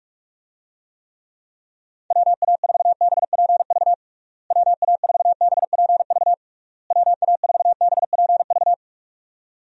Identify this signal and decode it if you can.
Morse code, heard as a keyed tone, about 46 wpm, 700 Hz. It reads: WA4BPV WA4BPV WA4BPV
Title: USA Call Signs - 6 Characters - Repeated 3x - 45wpm